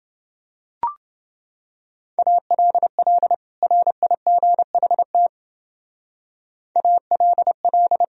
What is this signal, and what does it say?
Morse code, heard as a keyed tone, about 28 wpm, 700 Hz. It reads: ALL RIGHT ALL